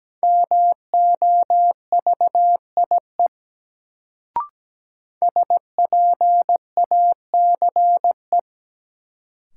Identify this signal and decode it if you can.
Morse code, heard as a keyed tone, about 17 wpm, 700 Hz. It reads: MOVIE SPACE
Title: Sentences from Top 1000 Words - Review Part 5 - New Word Drill - 17wpm